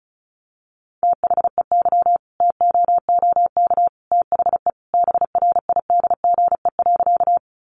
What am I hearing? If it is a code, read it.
Morse code, heard as a keyed tone, about 34 wpm, 700 Hz. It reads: THEY TOOK THE BRIDGE.
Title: Sentences from Top 1200 Words 35wpm